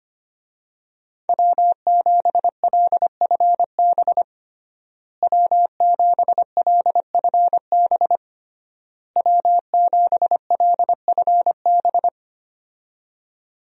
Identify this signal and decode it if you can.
Morse code, heard as a keyed tone, about 25 wpm, 700 Hz. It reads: W7LFB W7LFB W7LFB
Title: USA Call Signs - 5 Characters - Repeated 3x - 25wpm